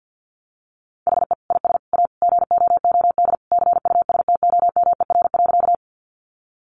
Morse code, heard as a keyed tone, about 50 wpm, 700 Hz. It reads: HE IS A GOOD CUSTOMER.
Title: Sentences from Top 1100 Words 50wpm